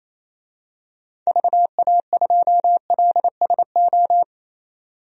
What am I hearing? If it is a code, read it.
Morse code, heard as a keyed tone, about 28 wpm, 700 Hz. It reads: VA2LSO
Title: DXCC-100 Call Signs - 6 Characters - 28wpm